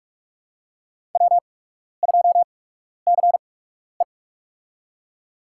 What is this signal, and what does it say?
Morse code, heard as a keyed tone, about 48 wpm, spaced as for 12 wpm, 700 Hz. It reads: W2CE
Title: USA Call Signs - 4 Characters - 8x Character Spacing - 45wpm